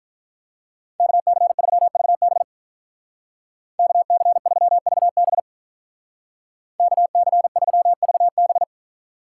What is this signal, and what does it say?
Morse code, heard as a keyed tone, about 44 wpm, 700 Hz. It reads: KC3VB KC3VB KC3VB
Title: Speed Racing - USA 2x2 Call Signs 35wpm